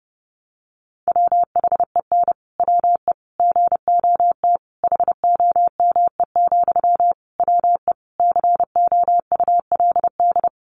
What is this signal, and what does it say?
Morse code, heard as a keyed tone, about 30 wpm, 700 Hz. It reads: WHEN WE GOT HOME, WE COULD
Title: Sentences from Top 1400 Words 30wpm